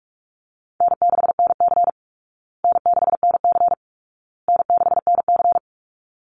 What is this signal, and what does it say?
Morse code, heard as a keyed tone, about 44 wpm, 700 Hz. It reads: N6NC N6NC N6NC
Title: Easy Call Signs Repeated 3x - 45wpm